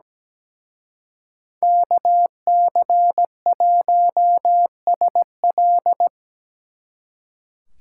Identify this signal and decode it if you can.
Morse code, heard as a keyed tone, about 20 wpm, 700 Hz. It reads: KC1SL
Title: Speed Racing - USA Call Signs 17wpm